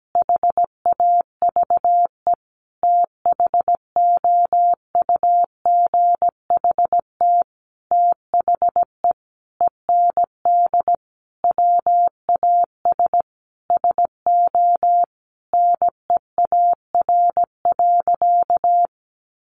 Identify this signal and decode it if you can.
Morse code, heard as a keyed tone, about 17 wpm, 700 Hz. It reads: HAVE THOUGHT THE END WAS SO NEAR.
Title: Sentences from Top 300 Words 17wpm